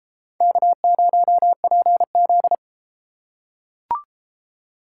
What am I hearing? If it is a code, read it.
Morse code, heard as a keyed tone, about 32 wpm, 700 Hz. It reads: K0PZ